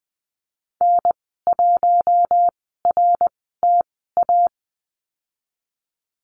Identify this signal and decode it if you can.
Morse code, heard as a keyed tone, about 20 wpm, spaced as for 15 wpm, 700 Hz. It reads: N1RTA
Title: 055 Lesson 10g - Letter r - Call Signs 20wpm